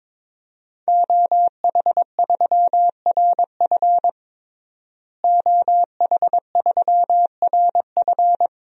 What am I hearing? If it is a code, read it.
Morse code, heard as a keyed tone, about 22 wpm, 700 Hz. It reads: OH3RF OH3RF
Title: DXCC-100 Call Signs - 5 Characters - Repeated 2x - 22wpm